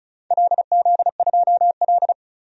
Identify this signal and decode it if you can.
Morse code, heard as a keyed tone, about 36 wpm, 700 Hz. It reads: LZ2L
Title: DXCC-100 Call Signs - 4 Characters - 35wpm